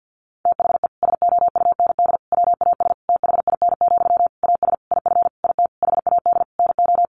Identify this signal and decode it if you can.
Morse code, heard as a keyed tone, about 50 wpm, 700 Hz. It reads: THE SOUND WAS THIN, AS IF IT HAD NO
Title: Sentences from Top 1200 Words 50wpm